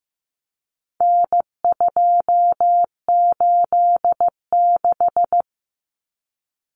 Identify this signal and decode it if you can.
Morse code, heard as a keyed tone, about 15 wpm, 700 Hz. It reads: N286
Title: Common Cut Numbers - 4-Digit 15wpm